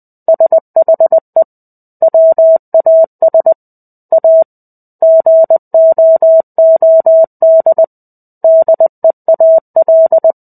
Morse code, heard as a keyed tone, about 20 wpm, 650 Hz. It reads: SHE WAS A GOOD DEAL
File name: 05-Chapter-05 20 WPM